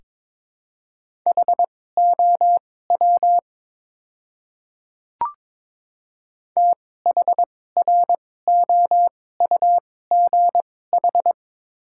Morse code, heard as a keyed tone, about 22 wpm, spaced as for 16 wpm, 700 Hz. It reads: HOW THROUGH
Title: 2x Character Spacing - Top 200 Words - 22wpm